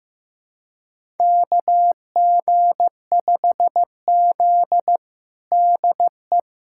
Morse code, heard as a keyed tone, about 15 wpm, 700 Hz. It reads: KG5Z DE
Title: USA Callsign DE Callsign - 4 Characters - 15wpm